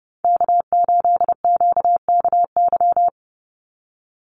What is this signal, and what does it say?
Morse code, heard as a keyed tone, about 30 wpm, 700 Hz. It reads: K8QKY
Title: Call Signs - 30wpm